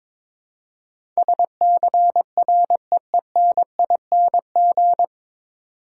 Morse code, heard as a keyed tone, about 21 wpm, 700 Hz. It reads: SCREENING
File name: 9 Letter Words - First 3 Letters Given - 22wpm